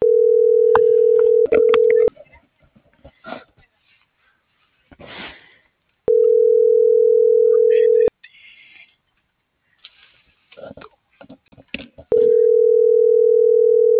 Ringback tone (North America), 440 + 480 Hz.